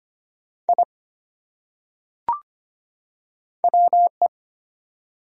Morse code, heard as a keyed tone, about 25 wpm, 700 Hz. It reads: I WE
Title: Top 100 Words 25wpm